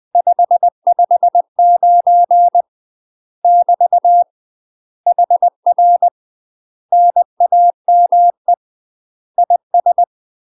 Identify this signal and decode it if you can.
Morse code, heard as a keyed tone, about 20 wpm, 700 Hz. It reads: 559 <BT> HR NAME IS
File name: UK MASTER QSOs TRAINING / 20WPM / UK MASTER QSOs 0040